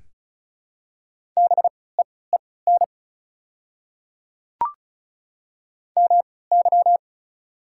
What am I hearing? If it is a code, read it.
Morse code, heard as a keyed tone, about 36 wpm, spaced as for 20 wpm, 690 Hz. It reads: BEEN MY
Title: 3x Character Spacing - Top 100 Words - 35wpm